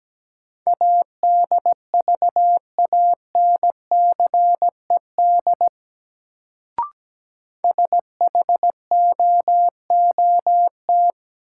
Morse code, heard as a keyed tone, about 17 wpm, 700 Hz. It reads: ADVANCED SHOOT
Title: Sentences from Top 1200 Words - New Word Drill Alt - 17wpm